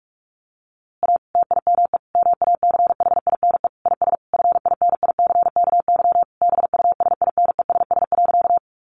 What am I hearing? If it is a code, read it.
Morse code, heard as a keyed tone, about 44 wpm, 700 Hz. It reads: A TIME MACHINE IS FINICKY BUSINESS.
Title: A Sound Of Thunder 45wpm